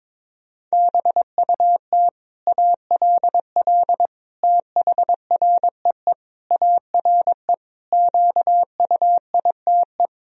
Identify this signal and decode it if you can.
Morse code, heard as a keyed tone, about 22 wpm, 700 Hz. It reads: BUT ALL THREE ARE QUITE
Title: Sentences from Top 1000 Words - Review Part 3 - 22wpm